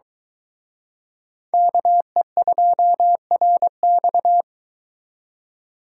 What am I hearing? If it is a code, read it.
Morse code, heard as a keyed tone, about 24 wpm, 700 Hz. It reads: KE2RX